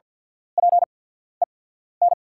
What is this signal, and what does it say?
Morse code, heard as a keyed tone, about 50 wpm, spaced as for 14 wpm, 700 Hz. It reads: PEN
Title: 8x Character Spacing - Top 500 Words - 50wpm